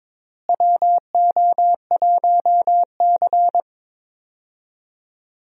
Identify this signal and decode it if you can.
Morse code, heard as a keyed tone, about 22 wpm, 700 Hz. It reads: WO1C
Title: USA Call Signs - 4 Characters - Repeated 3x - 22wpm